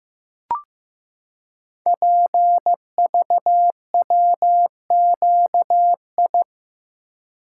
Morse code, heard as a keyed tone, about 15 wpm, 690 Hz. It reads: PVWQI
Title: Code Groups - 5 Letters - 15wpm